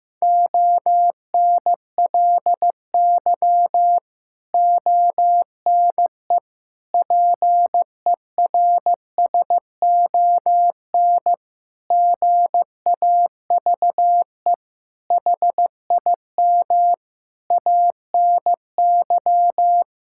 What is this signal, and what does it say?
Morse code, heard as a keyed tone, about 16 wpm, 700 Hz. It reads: ONLY ONE PERSON GAVE HIM ANY
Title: Sentences from Top 1000 Words - Review Part 6 - 15wpm